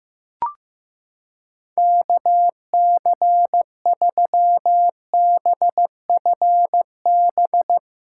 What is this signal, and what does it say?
Morse code, heard as a keyed tone, about 15 wpm, 700 Hz. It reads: KC3BFB
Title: USA Callsign DE Callsign - 6 Characters - 15wpm